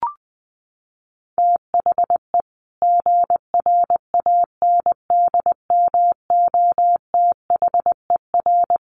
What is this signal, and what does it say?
Morse code, heard as a keyed tone, about 20 wpm, 700 Hz. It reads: THE GRANDMOTHER
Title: Sentences from Top 1200 Words 20wpm